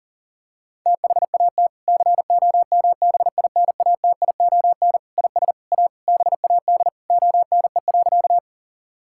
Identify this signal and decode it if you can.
Morse code, heard as a keyed tone, about 40 wpm, 700 Hz. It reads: THAT COMBINATION IS A BAD ONE.
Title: Sentences from Top 1000 Words - Review Part 6 - 40wpm